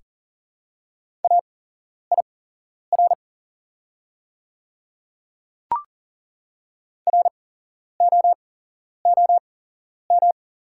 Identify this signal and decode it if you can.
Morse code, heard as a keyed tone, about 42 wpm, spaced as for 11 wpm, 690 Hz. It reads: AIR ROOM
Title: 8x Character Spacing - Top 300 Words - 40wpm